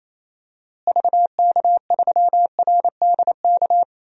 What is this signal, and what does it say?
Morse code, heard as a keyed tone, about 28 wpm, 700 Hz. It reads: VK3RDK
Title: DXCC-100 Call Signs - 6 Characters - Repeated 2x - 28wpm